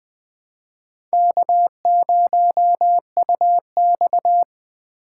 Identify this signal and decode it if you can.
Morse code, heard as a keyed tone, about 20 wpm, 700 Hz. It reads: K0UX
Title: USA Call Signs - 4 Characters - 2x Character Spacing - 20wpm